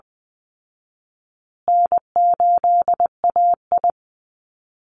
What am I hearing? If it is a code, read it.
Morse code, heard as a keyed tone, about 20 wpm, 700 Hz. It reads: N8AI
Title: Call Signs Repeated 2x - 20wpm